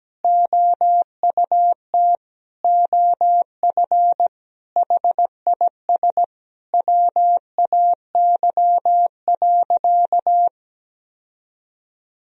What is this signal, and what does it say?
Morse code, heard as a keyed tone, about 17 wpm, 700 Hz. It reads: OUT OF HIS WAY.
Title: Sentences from Top 1000 Words - Review Part 1 - 17wpm